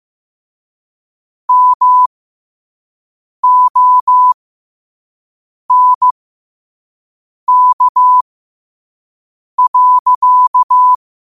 Morse code, heard as a keyed tone, about 15 wpm, spaced as for 5 wpm, 1000 Hz. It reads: MONK.